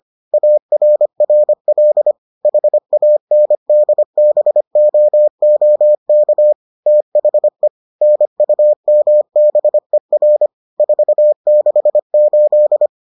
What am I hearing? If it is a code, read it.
Morse code, heard as a keyed tone, about 25 wpm, 600 Hz. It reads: ARRL HANDBOOK THE NUMBER 468